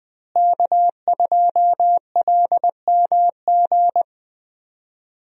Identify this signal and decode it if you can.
Morse code, heard as a keyed tone, about 21 wpm, 700 Hz. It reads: K2LMG
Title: USA Call Signs - 5 Characters - Repeated 3x - 20wpm